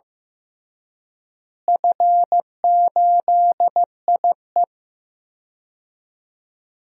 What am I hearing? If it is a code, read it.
Morse code, heard as a keyed tone, about 15 wpm, 700 Hz. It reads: F8IE
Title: DXCC-100 Call Signs - 4 Characters - Repeated 2x - 15wpm